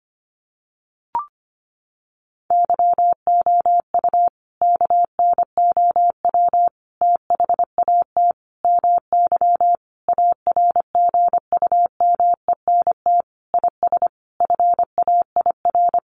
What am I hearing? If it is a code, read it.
Morse code, heard as a keyed tone, about 25 wpm, 700 Hz. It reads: YOU KNOW THAT MY ARGUMENT IS FAIR
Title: Sentences from Top 1400 Words 25wpm